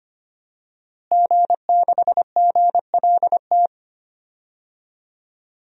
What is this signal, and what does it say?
Morse code, heard as a keyed tone, about 25 wpm, 700 Hz. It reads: G6GLT
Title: DXCC-100 Call Signs - 5 Characters - Repeated 2x - 25wpm